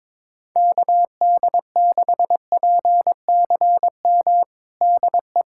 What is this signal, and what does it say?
Morse code, heard as a keyed tone, about 22 wpm, 700 Hz. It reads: KD6PCM DE
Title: USA Callsign DE Callsign 22wpm